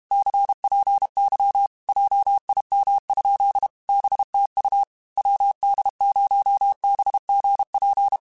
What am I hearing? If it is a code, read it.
Morse code, heard as a keyed tone, about 32 wpm, 800 Hz. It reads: CPY JIM? BTU WD0BGP